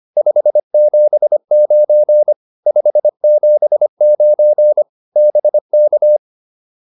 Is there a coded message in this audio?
Yes: Morse code from 0.2 to 6.2 s, heard as a keyed tone, about 25 wpm, 600 Hz. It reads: 579 579 BK